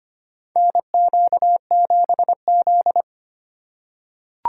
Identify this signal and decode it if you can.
Morse code, heard as a keyed tone, about 25 wpm, 700 Hz. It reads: NQ7Z
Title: USA Call Signs - 4 Characters - 3x Character Spacing - 25wpm